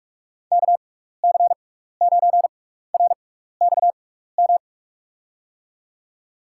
Morse code, heard as a keyed tone, about 41 wpm, spaced as for 15 wpm, 700 Hz. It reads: KC9RXM
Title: USA Call Signs - 6 Characters - 6x Character Spacing - 45wpm